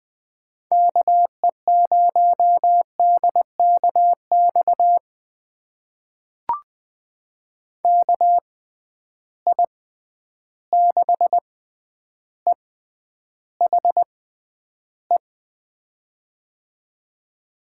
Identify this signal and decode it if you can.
Morse code, heard as a keyed tone, about 20 wpm, spaced as for 10 wpm, 700 Hz. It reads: KE0DKX KI6EHE
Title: USA Call Signs - 6 Characters - 6x Character Spacing - 20wpm